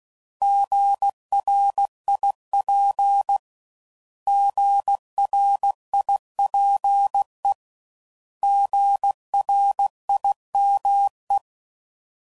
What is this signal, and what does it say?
Morse code, heard as a keyed tone, about 16 wpm, 800 Hz. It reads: GRIP GRIPE GRIME
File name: Prefixes / By First Letter / Gs